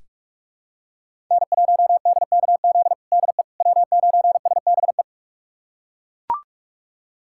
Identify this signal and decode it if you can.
Morse code, heard as a keyed tone, about 42 wpm, 700 Hz. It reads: N1DKZ DE W9SBE
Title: USA Callsign DE Callsign - 5 Characters - 45wpm